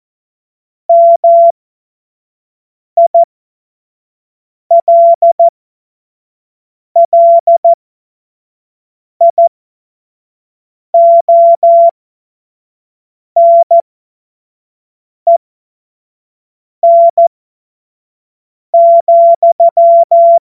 Morse code, heard as a keyed tone, about 14 wpm, spaced as for 5 wpm, 680 Hz. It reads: MILLIONEN,
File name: Morsekurs / Witze / Witz 10 Juni 2025 25 F70